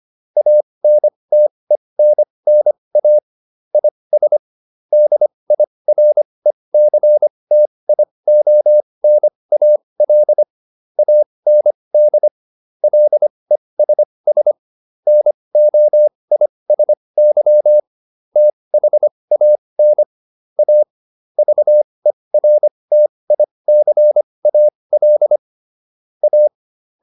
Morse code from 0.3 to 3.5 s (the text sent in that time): ANTENNA